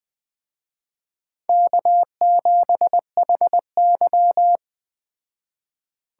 Morse code, heard as a keyed tone, about 21 wpm, 700 Hz. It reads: K7HY
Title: Speed Racing - USA 1x2 Call Signs 17wpm